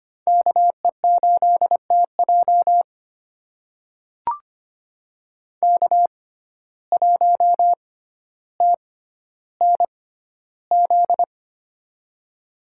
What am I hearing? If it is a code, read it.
Morse code, heard as a keyed tone, about 25 wpm, spaced as for 13 wpm, 700 Hz. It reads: KE8TJ K1TNZ
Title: USA Call Signs - 5 Characters - 6x Character Spacing - 25wpm